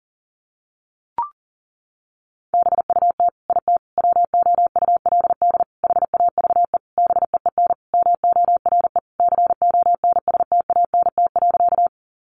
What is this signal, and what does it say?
Morse code, heard as a keyed tone, about 40 wpm, 700 Hz. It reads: BUT IT WOULD HAVE BEEN MORE CONSTANT.
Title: Sentences from Top 1500 Words 40wpm